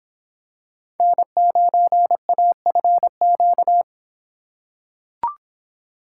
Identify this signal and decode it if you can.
Morse code, heard as a keyed tone, about 24 wpm, 700 Hz. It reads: N9AFQ